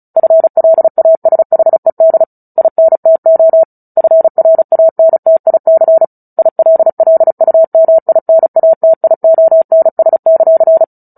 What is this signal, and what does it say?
Morse code, heard as a keyed tone, about 34 wpm, 650 Hz. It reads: FLASHED INTO FRANTIC ILLUMINATIONS;
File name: Book Two / 09-Chapter 9 - 35 WPM